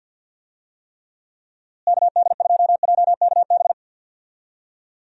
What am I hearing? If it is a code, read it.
Morse code, heard as a keyed tone, about 47 wpm, 700 Hz. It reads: KD2JKB
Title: USA Call Signs - 6 Characters - Repeated 2x - 50wpm